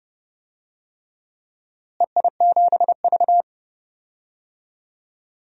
Morse code, heard as a keyed tone, about 32 wpm, 700 Hz. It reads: EI7V
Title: DXCC-100 Call Signs - 4 Characters - Repeated 3x - 30wpm